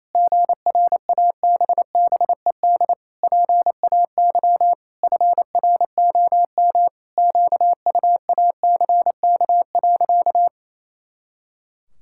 Morse code, heard as a keyed tone, about 27 wpm, 700 Hz. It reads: GRABBED PAY FROM QUACK.
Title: Pangram Challenge v2 - 28wpm